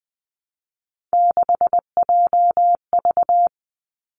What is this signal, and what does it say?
Morse code, heard as a keyed tone, about 19 wpm, 700 Hz. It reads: 6JV
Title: Three Challenging Letter-Number Combinations 20wpm